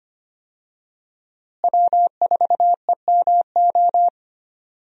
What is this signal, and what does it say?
Morse code, heard as a keyed tone, about 25 wpm, 700 Hz. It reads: W4EMO
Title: USA Call Signs - 5 Characters - Repeated 2x - 25wpm